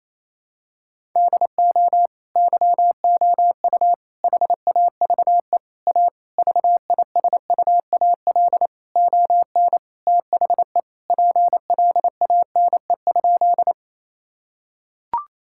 Morse code, heard as a keyed tone, about 28 wpm, 700 Hz. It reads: DO YOU HAVE A VISUAL ON THE PLANE?
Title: Sentences from Top 1000 Words - Review Part 6 - 28wpm